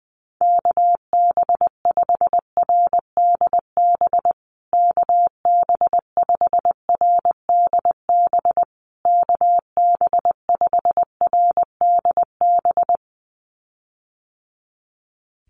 Morse code, heard as a keyed tone, about 20 wpm, 700 Hz. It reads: KB5RDB KB5RDB KB5RDB
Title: USA Call Signs Repeated 3x 20wpm